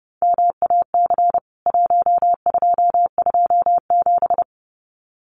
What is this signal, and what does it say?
Morse code, heard as a keyed tone, about 30 wpm, 700 Hz. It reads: MAC 1227
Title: CWT Exchange 30wpm